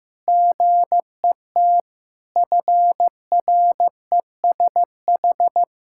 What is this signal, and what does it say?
Morse code, heard as a keyed tone, about 15 wpm, 700 Hz. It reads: GET FRESH
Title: Sentences from Top 1600 Words 15wpm